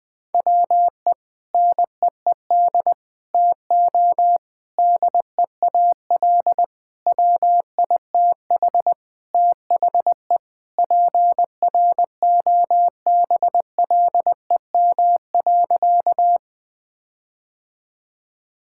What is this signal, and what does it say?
Morse code, heard as a keyed tone, about 21 wpm, 700 Hz. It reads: WE NEED TO DEAL WITH THE PROBLEM.
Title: Sentences with 4-Word Phrases - 20wpm